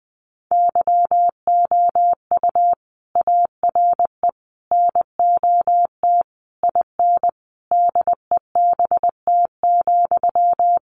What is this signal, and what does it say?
Morse code, heard as a keyed tone, about 20 wpm, 700 Hz. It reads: YOU ARE NOT IN DEBT,
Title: Sentences from Top 1300 Words 20wpm